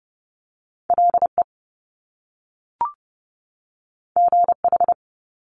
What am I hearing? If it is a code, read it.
Morse code, heard as a keyed tone, about 30 wpm, 700 Hz. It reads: LE GH
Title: Top Two Letter Combinations in Words 30wpm